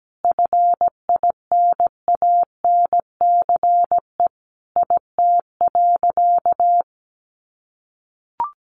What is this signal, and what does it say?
Morse code, heard as a keyed tone, about 17 wpm, 700 Hz. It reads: FINANCE IT.
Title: Sentences from Top 1500 Words 17wpm